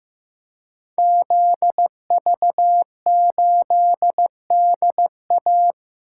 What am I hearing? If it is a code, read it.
Morse code, heard as a keyed tone, about 15 wpm, 700 Hz. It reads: ZV8DA
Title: Code Groups - 5 Letter:Numbers - 15wpm